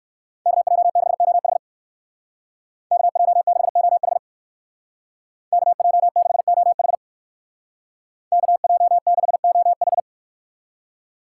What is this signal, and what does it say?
Morse code, heard as a keyed tone, about 59 wpm, 700 Hz. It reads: KJ6OH KJ6OH KJ6OH KJ6OH